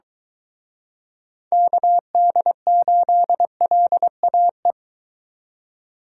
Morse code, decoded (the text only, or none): KD8LAE